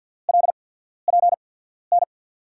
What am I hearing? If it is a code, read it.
Morse code, heard as a keyed tone, about 50 wpm, spaced as for 14 wpm, 700 Hz. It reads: LPN
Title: USA Call Signs - 5 Characters - 8x Character Spacing - 50wpm